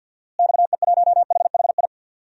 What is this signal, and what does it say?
Morse code, heard as a keyed tone, about 47 wpm, 700 Hz. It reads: XE1HHI
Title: DXCC-100 Call Signs - 6 Characters - Repeated 3x - 50wpm